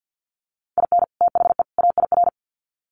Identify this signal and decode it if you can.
Morse code, heard as a keyed tone, about 50 wpm, 700 Hz. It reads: IN THE AIR
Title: Sets of 3 Words 50wpm